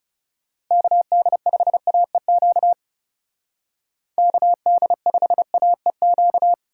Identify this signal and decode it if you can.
Morse code, heard as a keyed tone, about 38 wpm, 700 Hz. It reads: KD5AEQ KD5AEQ
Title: Speed Racing - USA 2x3 Call Signs 30wpm